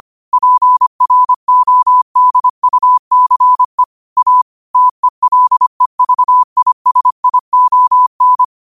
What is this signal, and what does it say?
Morse code, heard as a keyed tone, about 25 wpm, 1000 Hz. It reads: PRODUCE A TELEVISION
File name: qotdfiles / 25wpm